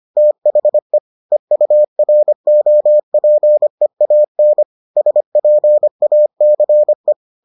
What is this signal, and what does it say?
Morse code, heard as a keyed tone, about 25 wpm, 600 Hz. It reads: THE EUROPEAN SPACE